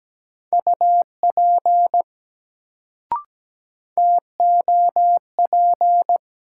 Morse code, heard as a keyed tone, about 17 wpm, 700 Hz. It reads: UP TOP
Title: Sets of 2 Words - New Word Drill - 17wpm